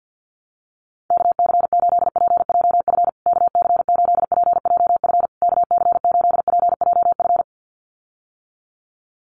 Morse code, heard as a keyed tone, about 50 wpm, 700 Hz. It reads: KC8PJF KC8PJF KC8PJF
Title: USA Call Signs Repeated 3x 50wpm